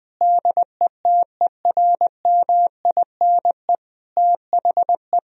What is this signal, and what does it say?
Morse code, heard as a keyed tone, about 19 wpm, 700 Hz. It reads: DETERMINE THE
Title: Sentences from Top 1000 Words - Review Part 4 - 20wpm